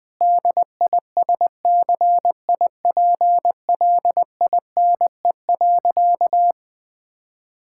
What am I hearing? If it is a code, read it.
Morse code, heard as a keyed tone, about 21 wpm, 700 Hz. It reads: DISCIPLINE.
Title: Sentences from Top 1000 Words - Review Part 6 - 20wpm